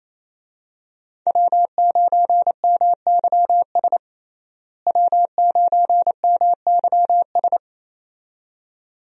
Morse code, heard as a keyed tone, about 28 wpm, 700 Hz. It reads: W9MYS W9MYS
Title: USA Call Signs - 5 Characters - Repeated 2x - 28wpm